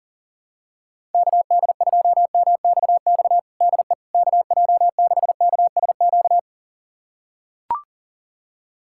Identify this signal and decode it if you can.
Morse code, heard as a keyed tone, about 42 wpm, 700 Hz. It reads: KD2MXX DE KJ6KSQ
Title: USA Callsign DE Callsign 40wpm